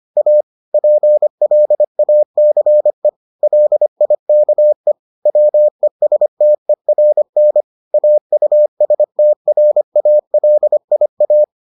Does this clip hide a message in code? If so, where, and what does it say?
Yes: Morse code from 0.2 to 11.4 s, heard as a keyed tone, about 25 wpm, 600 Hz. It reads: A PLACE LIKE WESTERN AUSTRALIA